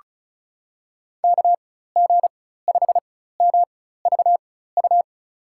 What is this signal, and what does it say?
Morse code, heard as a keyed tone, about 34 wpm, spaced as for 16 wpm, 700 Hz. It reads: KG5MVU
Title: USA Call Signs - 6 Characters - 4x Character Spacing - 35wpm